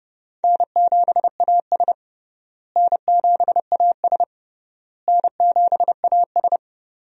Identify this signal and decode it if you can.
Morse code, heard as a keyed tone, about 29 wpm, 700 Hz. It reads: N7AS N7AS N7AS
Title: USA Call Signs - 4 Characters - Repeated 3x - 30wpm